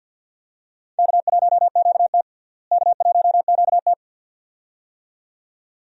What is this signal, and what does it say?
Morse code, heard as a keyed tone, about 49 wpm, 700 Hz. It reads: K1QT K1QT
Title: USA Call Signs - 4 Characters - Repeated 2x - 50wpm